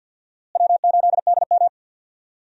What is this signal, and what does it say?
Morse code, heard as a keyed tone, about 51 wpm, 700 Hz. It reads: W8DM
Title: USA Call Signs - 4 Characters - Repeated 3x - 50wpm